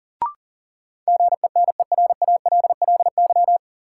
Morse code, heard as a keyed tone, about 40 wpm, 700 Hz. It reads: GENERALLY
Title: Sentences from Top 600 Words - New Word Drill - 40wpm